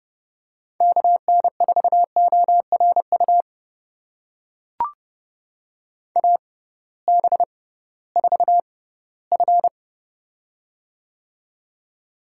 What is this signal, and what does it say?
Morse code, heard as a keyed tone, about 30 wpm, spaced as for 17 wpm, 700 Hz. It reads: KN4ORU AB4F
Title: USA Call Signs - 6x Character Spacing - 30wpm